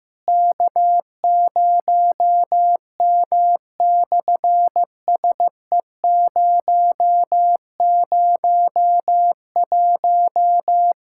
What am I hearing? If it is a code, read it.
Morse code, heard as a keyed tone, about 15 wpm, 700 Hz. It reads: K0M/SE001
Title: SOTA QSO Practice 15wpm